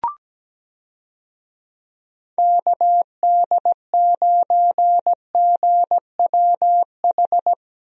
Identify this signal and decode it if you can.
Morse code, heard as a keyed tone, about 17 wpm, 700 Hz. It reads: KD9GWH
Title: USA Call Signs - 6 Characters - Repeated 2x - 17wpm